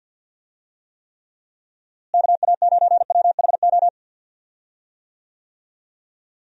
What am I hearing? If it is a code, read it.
Morse code, heard as a keyed tone, about 49 wpm, 700 Hz. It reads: KA9WHO
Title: USA Call Signs - 6 Characters - Repeated 2x - 50wpm